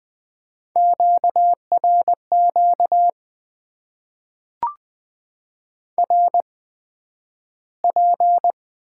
Morse code, heard as a keyed tone, about 20 wpm, spaced as for 11 wpm, 700 Hz. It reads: QRQ RP
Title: Top 90 QSO Elements - 8x Character Spacing - 20wpm